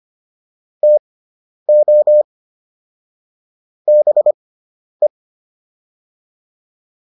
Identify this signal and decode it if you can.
Morse code, heard as a keyed tone, about 25 wpm, spaced as for 10 wpm, 600 Hz. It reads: TO BE